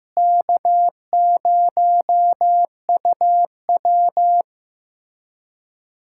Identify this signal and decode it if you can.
Morse code, heard as a keyed tone, about 15 wpm, 700 Hz. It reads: K0UW
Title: USA Callsign DE Callsign - 4 Characters - 15wpm